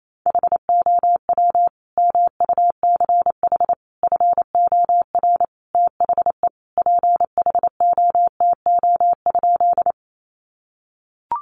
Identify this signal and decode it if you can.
Morse code, heard as a keyed tone, about 28 wpm, 700 Hz. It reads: HOW MUCH FOR THE PHOTO?
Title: Sentences from Top 1200 Words 28wpm